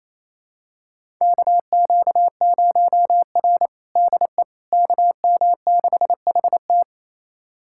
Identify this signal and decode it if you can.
Morse code, heard as a keyed tone, about 28 wpm, 700 Hz. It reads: KQ0R DE KM6HT
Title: USA Callsign DE Callsign 28wpm